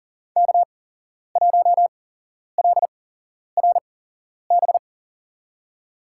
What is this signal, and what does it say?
Morse code, heard as a keyed tone, about 40 wpm, spaced as for 11 wpm, 700 Hz. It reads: K1LRB
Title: USA Call Signs - 5 Characters - 8x Character Spacing - 40wpm